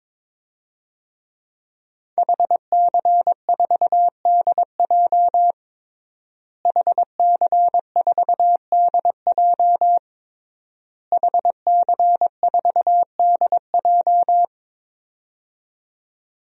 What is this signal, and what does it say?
Morse code, heard as a keyed tone, about 22 wpm, 700 Hz. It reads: HC4DJ HC4DJ HC4DJ
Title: DXCC-100 Call Signs - 5 Characters - Repeated 3x - 22wpm